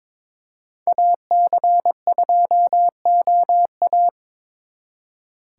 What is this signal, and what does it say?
Morse code, heard as a keyed tone, about 26 wpm, 700 Hz. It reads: AC2OA
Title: Speed Racing - USA Call Signs 22wpm